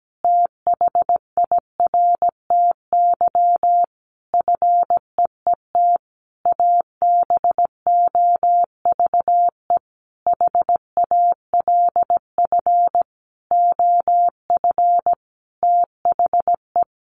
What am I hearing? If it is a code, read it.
Morse code, heard as a keyed tone, about 17 wpm, 700 Hz. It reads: THIRTY FEET ABOVE HALF OF THE
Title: A Sound Of Thunder 17wpm